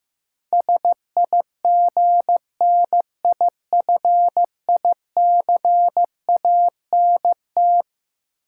Morse code, heard as a keyed tone, about 16 wpm, 700 Hz. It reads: SIGNIFICANT
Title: Sentences from Top 1000 Words - Review Part 6 - New Word Drill - 15wpm